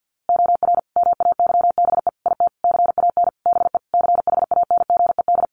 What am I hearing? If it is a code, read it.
Morse code, heard as a keyed tone, about 50 wpm, 700 Hz. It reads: OR MAYBE IT CAN BE CHANGED
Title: A Sound Of Thunder 50wpm